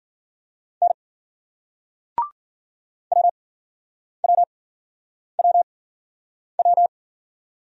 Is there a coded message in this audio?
Yes: Morse code from 0.8 to 6.9 s, heard as a keyed tone, about 56 wpm, spaced as for 9 wpm, 700 Hz. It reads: N WWWW